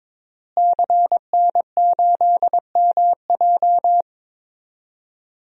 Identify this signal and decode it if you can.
Morse code, heard as a keyed tone, about 22 wpm, 700 Hz. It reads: CN8MJ
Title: DXCC-100 Call Signs - 5 Characters - Repeated 2x - 22wpm